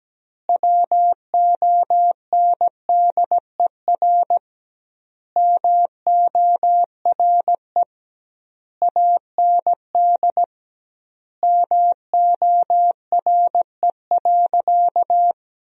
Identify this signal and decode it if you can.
Morse code, heard as a keyed tone, about 17 wpm, 700 Hz. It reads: WONDER MORE AND MORE.
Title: 2x Word-Spacing - Sentences from Top 500 Words - 17wpm